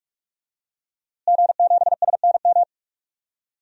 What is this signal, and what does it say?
Morse code, heard as a keyed tone, about 41 wpm, 700 Hz. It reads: G7SNM
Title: DXCC-100 Call Signs - 5 Characters - Repeated 2x - 45wpm